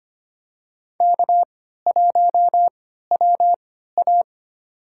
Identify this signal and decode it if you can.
Morse code, heard as a keyed tone, about 25 wpm, spaced as for 14 wpm, 700 Hz. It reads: K1WA
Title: USA Call Signs - 4 Characters - 3x Character Spacing - 25wpm